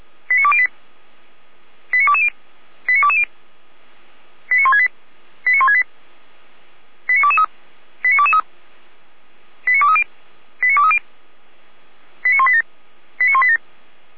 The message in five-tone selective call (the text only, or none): 89208 89220 89220 89168 89168 89223 89223 89230 89230 89187 89187